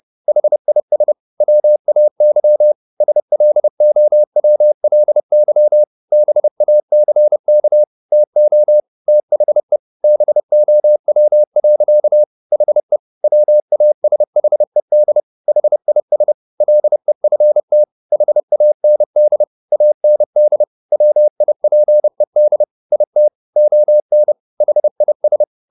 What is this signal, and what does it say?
Morse code, heard as a keyed tone, about 30 wpm, 600 Hz. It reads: HIS WAY SLOWLY BACK TO THE BOW. HE WASHED HIS LEFT HAND AND WIPED IT ON HIS